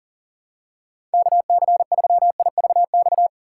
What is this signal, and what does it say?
Morse code, heard as a keyed tone, about 37 wpm, 700 Hz. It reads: KC3IVX
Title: USA Call Signs - 6 Characters - 2x Character Spacing - 40wpm